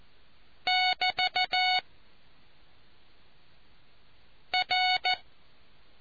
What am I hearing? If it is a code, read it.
Morse code, heard as a keyed tone, about 14 wpm, 740 Hz. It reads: <BT> R